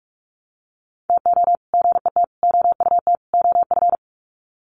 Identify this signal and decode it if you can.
Morse code, heard as a keyed tone, about 44 wpm, 700 Hz. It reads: TO GET OUT OF
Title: Sets of 4 Words 45wpm